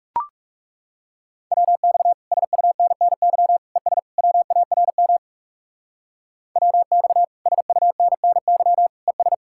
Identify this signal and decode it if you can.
Morse code, heard as a keyed tone, about 35 wpm, 700 Hz. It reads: WX SUNNY ES WARM WX SUNNY ES
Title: Speed-Racing Common QSO Phrases Encore - 30wpm